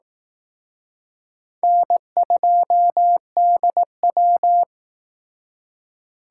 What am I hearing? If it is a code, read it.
Morse code, heard as a keyed tone, about 19 wpm, 700 Hz. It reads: N2DW
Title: Speed Racing - USA 1x2 Call Signs 15wpm